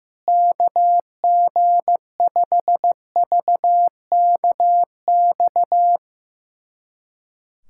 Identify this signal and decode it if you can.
Morse code, heard as a keyed tone, about 15 wpm, 700 Hz. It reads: KG5VKX
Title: USA Callsign DE Callsign - 6 Characters - 15wpm